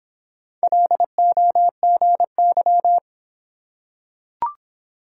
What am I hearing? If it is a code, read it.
Morse code, heard as a keyed tone, about 24 wpm, 700 Hz. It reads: LOGY